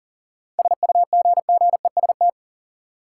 Morse code, heard as a keyed tone, about 40 wpm, 700 Hz. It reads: SUGGEST
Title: Sentences from Top 1300 Words - New Word Drill - 40wpm